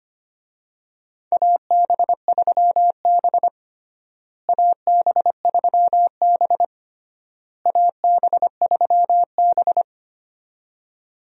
Morse code, heard as a keyed tone, about 25 wpm, 700 Hz. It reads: AB3B AB3B AB3B
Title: USA Call Signs - 4 Characters - Repeated 3x - 25wpm